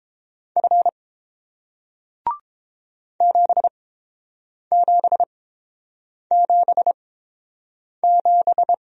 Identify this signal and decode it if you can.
Morse code, heard as a keyed tone, about 30 wpm, spaced as for 7 wpm, 700 Hz. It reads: F 7777